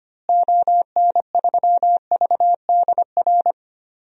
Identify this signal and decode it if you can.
Morse code, heard as a keyed tone, about 25 wpm, 700 Hz. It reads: ON3VDR
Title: DXCC-100 Call Signs - 6 Characters - Repeated 2x - 25wpm